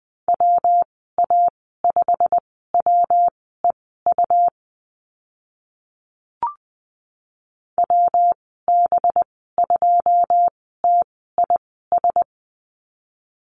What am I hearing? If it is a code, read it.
Morse code, heard as a keyed tone, about 20 wpm, spaced as for 14 wpm, 700 Hz. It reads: WA5WEU WB2TIS
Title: 186 Lesson 30g - Letter b - Call Signs 20wpm